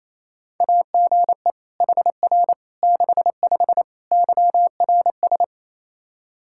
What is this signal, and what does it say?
Morse code, heard as a keyed tone, about 28 wpm, 700 Hz. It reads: AGE HR 65 YRS
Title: Practice QSO - Exchange 3 - More Info - 28wpm